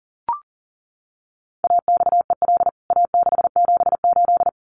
Morse code, heard as a keyed tone, about 40 wpm, 700 Hz. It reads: AXEL A678
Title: CWT Exchange 40wpm